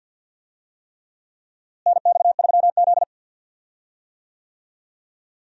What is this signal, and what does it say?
Morse code, heard as a keyed tone, about 52 wpm, 700 Hz. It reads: NX3Z
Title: USA Call Signs - 4 Characters - Repeated 3x - 50wpm